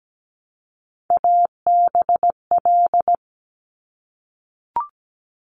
Morse code, heard as a keyed tone, about 17 wpm, 700 Hz. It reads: ABL
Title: Top Three Letter Combinations in Words 17wpm